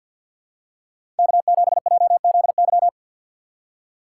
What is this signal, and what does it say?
Morse code, heard as a keyed tone, about 48 wpm, 690 Hz. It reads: K7JZY
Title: USA Call Signs - 5 Characters - 2x Character Spacing - 50wpm